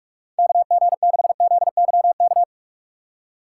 Morse code, heard as a keyed tone, about 46 wpm, 700 Hz. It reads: KG6ZYK
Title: USA Call Signs - 6 Characters - 3x Character Spacing - 45wpm